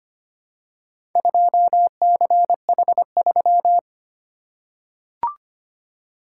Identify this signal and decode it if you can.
Morse code, heard as a keyed tone, about 25 wpm, 700 Hz. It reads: 2CH3
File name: practice / 25wpm / Four Letter-Number Combinations - 25wpm